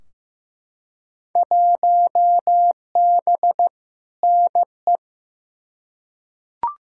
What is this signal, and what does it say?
Morse code, heard as a keyed tone, about 16 wpm, 700 Hz. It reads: 1B NE
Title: Field Day Exchanges Encore - 15wpm